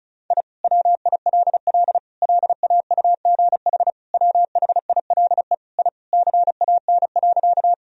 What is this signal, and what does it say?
Morse code, heard as a keyed tone, about 34 wpm, 700 Hz. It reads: I WILL LAUGH WHILE I CAN.
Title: Sentences from Top 1000 Words - Review Part 1 - 35wpm